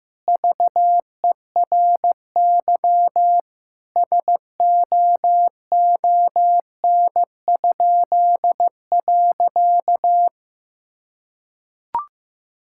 Morse code, heard as a keyed tone, about 15 wpm, 700 Hz. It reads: VERY SOON?.
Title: Pangram Challenge v2 - 15wpm